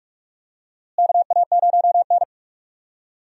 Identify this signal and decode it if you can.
Morse code, heard as a keyed tone, about 48 wpm, 700 Hz. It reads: KA0N
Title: USA Call Signs - 4 Characters - Repeated 3x - 45wpm